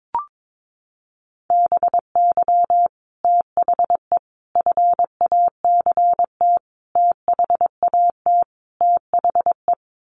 Morse code, heard as a keyed tone, about 22 wpm, 700 Hz. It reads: BY THE FACT THAT THE
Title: Sets of 5 Words Encore 22wpm